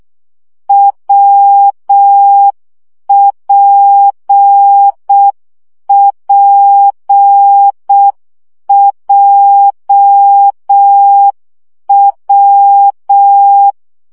Morse code, heard as a keyed tone, about 6 wpm, 800 Hz. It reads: WPPJW
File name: CW gyakorlas w p j 5 baud